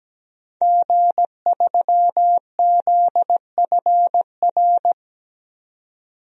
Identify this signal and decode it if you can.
Morse code, heard as a keyed tone, about 17 wpm, 700 Hz. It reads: G3ZFR
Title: DXCC-100 Call Signs - 5 Characters - 17wpm